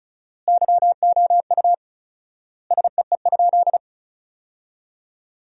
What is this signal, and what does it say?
Morse code, heard as a keyed tone, about 34 wpm, 700 Hz. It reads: YOU SEE?
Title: 4x Word-Spacing - Sentences from Top 100 Words - 35wpm